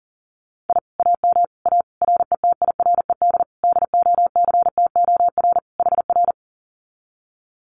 Morse code, heard as a keyed tone, about 39 wpm, 700 Hz. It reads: I AM A RETIRED DOCTOR HR
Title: Common QSO Phrases Encore 40wpm